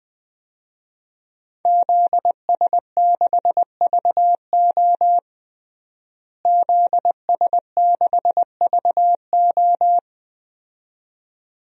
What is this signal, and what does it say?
Morse code, heard as a keyed tone, about 21 wpm, 700 Hz. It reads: ZS6VO ZS6VO
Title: DXCC-100 Call Signs - 5 Characters - Repeated 2x - 20wpm